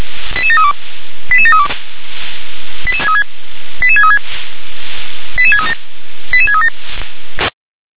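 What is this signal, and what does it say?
Five-tone selective call (ZVEI), heard as quick runs of five tones, tones of about 70 ms. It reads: 88732 88732 88636 88637 88637
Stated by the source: ZVEI five-tone selective call received by SDR.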